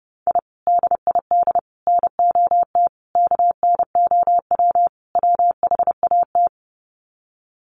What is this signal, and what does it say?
Morse code, heard as a keyed tone, about 30 wpm, 700 Hz. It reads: I DID NOT KNOW WHAT
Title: Sets of 5 Words 30wpm